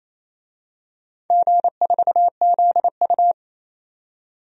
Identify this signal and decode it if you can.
Morse code, heard as a keyed tone, about 28 wpm, 700 Hz. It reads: G4ZU
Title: DXCC-100 Call Signs - 4 Characters - 28wpm